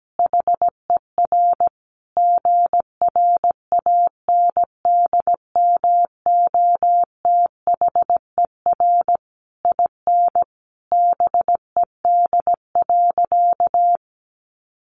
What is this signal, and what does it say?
Morse code, heard as a keyed tone, about 17 wpm, 700 Hz. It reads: HER GRANDMOTHER IN BED.
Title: Sentences from Top 1200 Words 17wpm